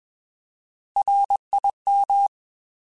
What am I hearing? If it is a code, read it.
Morse code, heard as a keyed tone, about 21 wpm, 800 Hz. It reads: RIM
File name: Alphabetical / Alphabetical mp3s / Rs @20